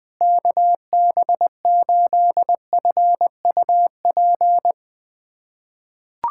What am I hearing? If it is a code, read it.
Morse code, heard as a keyed tone, about 20 wpm, 700 Hz. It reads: KB8FUP
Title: USA Callsign DE Callsign 20wpm